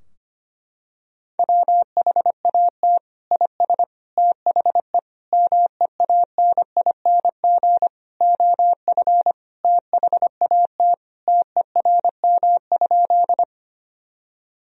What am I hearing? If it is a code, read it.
Morse code, heard as a keyed tone, about 25 wpm, 700 Hz. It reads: WHAT IS THE MEANING OF THAT TERM?
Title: Sentences from Top 1000 Words - Review Part 4 - 25wpm